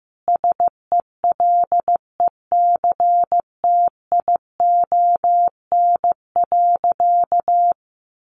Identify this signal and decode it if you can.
Morse code, heard as a keyed tone, about 15 wpm, 700 Hz. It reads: SELECTION.
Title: Sentences from Top 1400 Words 15wpm